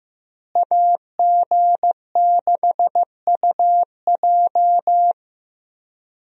Morse code, heard as a keyed tone, about 15 wpm, 700 Hz. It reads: AG6UJ
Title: USA Call Signs - 5 Characters - Repeated 2x - 15wpm